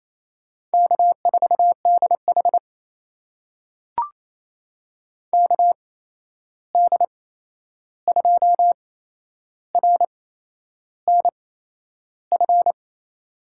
Morse code, heard as a keyed tone, about 28 wpm, spaced as for 11 wpm, 700 Hz. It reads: K4DH KD2RNF
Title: USA Call Signs - 8x Character Spacing - 28wpm